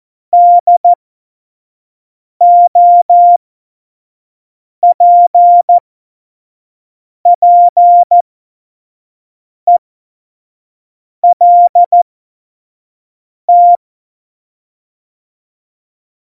Morse code, heard as a keyed tone, about 14 wpm, spaced as for 5 wpm, 700 Hz. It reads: DOPPELT
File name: Morsekurs / Witze / Witz 06. Februar 2018, 25 F70